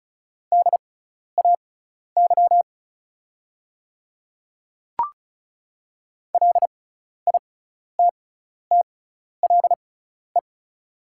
Morse code, heard as a keyed tone, about 36 wpm, spaced as for 12 wpm, 690 Hz. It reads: DAY LITTLE
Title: 6x Character Spacing - Top 300 Words - 35wpm